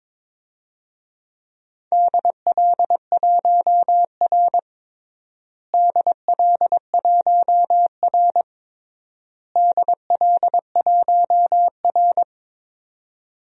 Morse code, heard as a keyed tone, about 22 wpm, 700 Hz. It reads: DL1R DL1R DL1R
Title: DXCC-100 Call Signs - 4 Characters - Repeated 3x - 22wpm